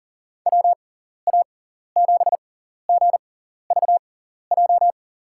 Morse code, heard as a keyed tone, about 40 wpm, spaced as for 14 wpm, 700 Hz. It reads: WA7GVJ
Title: USA Call Signs - 6x Character Spacing - 40wpm